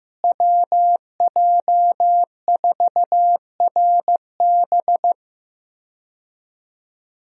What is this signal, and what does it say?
Morse code, heard as a keyed tone, about 15 wpm, 700 Hz. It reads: WJ4RB
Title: USA Call Signs - 5 Characters - Repeated 3x - 15wpm